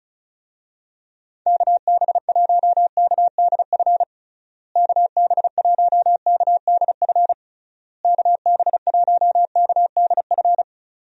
Morse code, heard as a keyed tone, about 34 wpm, 700 Hz. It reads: KB1KDF KB1KDF KB1KDF
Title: USA Call Signs - 6 Characters - Repeated 3x - 35wpm